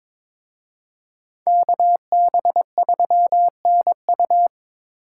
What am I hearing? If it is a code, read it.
Morse code, heard as a keyed tone, about 26 wpm, 700 Hz. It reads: KB3NU
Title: Speed Racing - USA 2x2 Call Signs 22wpm